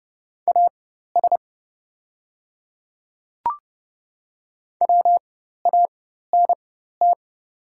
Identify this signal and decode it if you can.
Morse code, heard as a keyed tone, about 30 wpm, spaced as for 14 wpm, 700 Hz. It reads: AS WANT
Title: 4x Character Spacing - Top 300 Words - 30wpm